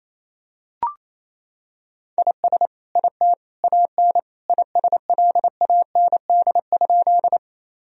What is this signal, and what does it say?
Morse code, heard as a keyed tone, about 27 wpm, 700 Hz. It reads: IS IT AN ISLAND?
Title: Sentences from Top 1000 Words - Review Part 3 - 28wpm